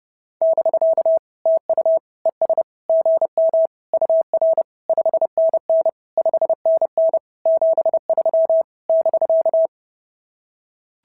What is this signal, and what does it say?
Morse code, heard as a keyed tone, about 28 wpm, 690 Hz. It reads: <BK> TU ES GM UR 5NN 5NN 73 <BK>
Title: SOTA QSO Practice 30wpm